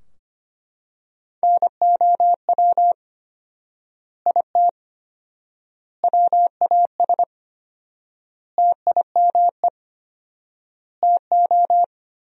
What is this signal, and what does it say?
Morse code, heard as a keyed tone, about 25 wpm, 700 Hz. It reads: NOW IT WAS TIME TO
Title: 4x Word-Spacing - Sentences from Top 200 Words - 25wpm